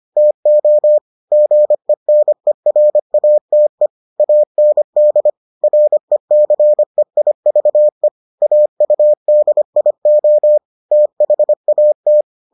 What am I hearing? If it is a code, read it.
Morse code, heard as a keyed tone, about 25 wpm, 600 Hz. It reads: TO GENERATE AND RECEIVE AUDIO THAT